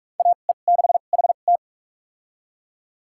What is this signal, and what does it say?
Morse code, heard as a keyed tone, about 42 wpm, spaced as for 32 wpm, 700 Hz. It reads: AE6HT
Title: USA Call Signs - 2x Character Spacing - 45wpm